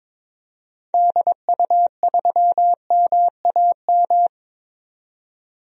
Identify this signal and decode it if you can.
Morse code, heard as a keyed tone, about 22 wpm, 700 Hz. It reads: DU3MAM
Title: DXCC-100 Call Signs - 6 Characters - 22wpm